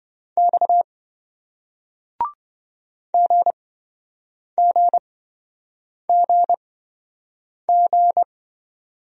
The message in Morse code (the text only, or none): X GGGG